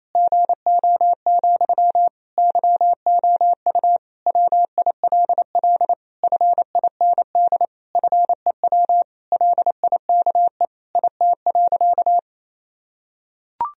Morse code, heard as a keyed tone, about 28 wpm, 700 Hz. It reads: GO, YOU WILL FIND FEW LIKE IT.
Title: Sentences from Top 1000 Words - Review Part 1 - 28wpm